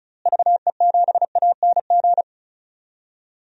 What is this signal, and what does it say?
Morse code, heard as a keyed tone, about 36 wpm, 700 Hz. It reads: VE7ANG
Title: DXCC-100 Call Signs - 6 Characters - Repeated 3x - 35wpm